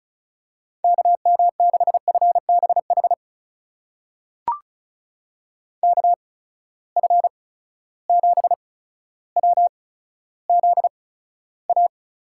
Morse code, heard as a keyed tone, about 36 wpm, spaced as for 15 wpm, 700 Hz. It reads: KM6FBH KF7WZA
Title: USA Call Signs - 6 Characters - 8x Character Spacing - 35wpm